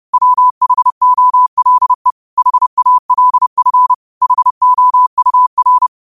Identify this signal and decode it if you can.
Morse code, heard as a keyed tone, about 30 wpm, 1000 Hz. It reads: WHOLE HALF HOUR
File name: 30wpm